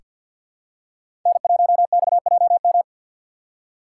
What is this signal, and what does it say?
Morse code, heard as a keyed tone, about 46 wpm, 700 Hz. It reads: N1XJM
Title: K1USN SST QSO Practice - 50wpm